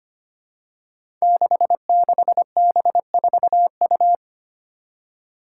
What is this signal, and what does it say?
Morse code, heard as a keyed tone, about 25 wpm, 700 Hz. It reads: 66B4U
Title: Five Challenging Letter-Number Combinations 25wpm